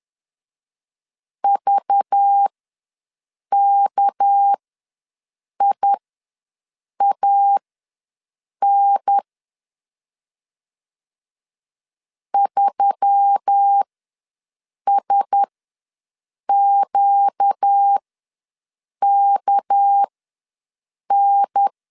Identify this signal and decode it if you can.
Morse code, heard as a keyed tone, about 10 wpm, spaced as for 6 wpm, 800 Hz. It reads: VKIAN 3SQKN
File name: LEZIONI / 6 PKX0 / 6 PK ws 10wpm 3745